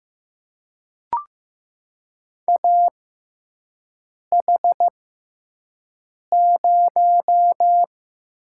Morse code, heard as a keyed tone, about 15 wpm, spaced as for 5 wpm, 700 Hz. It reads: AH0
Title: USA Call Signs - 4 Characters - 6x Character Spacing - 15wpm